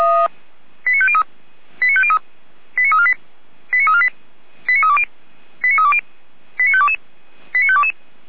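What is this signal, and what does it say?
Five-tone selective call (ZVEI), heard as quick runs of five tones, tones of about 70 ms. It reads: 89503 89503 89348 89348 89320 89320 89522 89522
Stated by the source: ZVEI five-tone selective call 89150 received by SDR.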